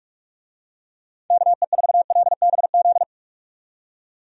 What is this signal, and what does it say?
Morse code, heard as a keyed tone, about 46 wpm, 700 Hz. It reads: KE4LBZ
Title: USA Call Signs - 6 Characters - 45wpm